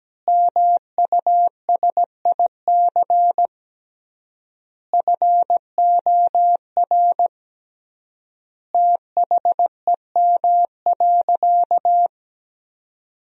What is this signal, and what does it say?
Morse code, heard as a keyed tone, about 17 wpm, 700 Hz. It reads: MUSIC FOR THEM.
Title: 3x Word-Spacing - Sentences from Top 300 Words - 17wpm